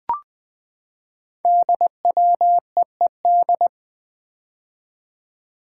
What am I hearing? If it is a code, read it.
Morse code, heard as a keyed tone, about 20 wpm, 700 Hz. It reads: DWEED
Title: Code Groups - 5 Letter:Numbers - 20wpm